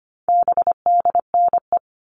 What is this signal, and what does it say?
Morse code, heard as a keyed tone, about 25 wpm, 700 Hz. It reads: BDNE
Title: All Cut Numbers - 4-Digit 25wpm